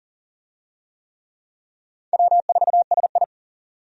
Morse code, heard as a keyed tone, about 40 wpm, 700 Hz. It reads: W4SI
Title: USA Call Signs - 4 Characters - 40wpm